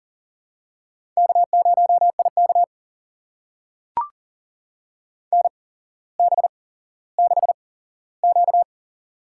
Morse code, heard as a keyed tone, about 39 wpm, spaced as for 17 wpm, 700 Hz. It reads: K0IK NB6Q
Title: USA Call Signs - 4 Characters - 8x Character Spacing - 40wpm